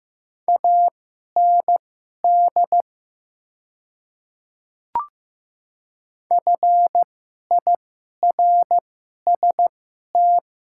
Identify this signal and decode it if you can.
Morse code, heard as a keyed tone, about 15 wpm, spaced as for 11 wpm, 700 Hz. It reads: AND FIRST
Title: 2x Character Spacing - Top 300 Words - 15wpm